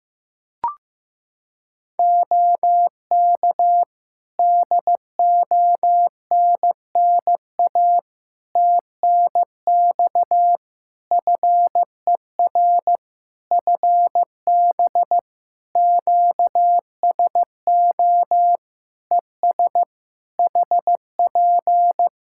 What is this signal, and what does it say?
Morse code, heard as a keyed tone, about 15 wpm, 700 Hz. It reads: OK DONNA TNX FER FB QSO ES HP
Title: Practice QSO - Exchange 4 - Ending - 15wpm